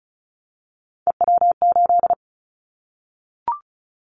Morse code, heard as a keyed tone, about 36 wpm, 700 Hz. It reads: EW8
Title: Callsign Prefixes for Top 50 DXCC - 35wpm